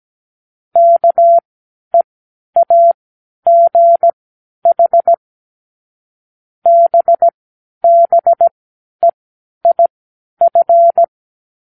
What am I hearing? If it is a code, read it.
Morse code, heard as a keyed tone, about 17 wpm, spaced as for 10 wpm, 680 Hz. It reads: KEAGH BBEIF
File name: k7qo / Lesson18(A-L)